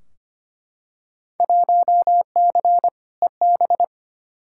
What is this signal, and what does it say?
Morse code, heard as a keyed tone, about 25 wpm, 700 Hz. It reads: 1C EB
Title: Field Day Exchanges Encore - 25wpm